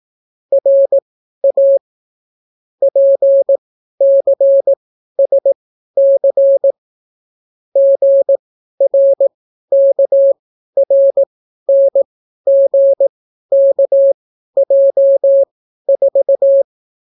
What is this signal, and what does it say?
Morse code, heard as a keyed tone, about 18 wpm, spaced as for 12 wpm, 550 Hz. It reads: RA PCSC GRKRNGKJ4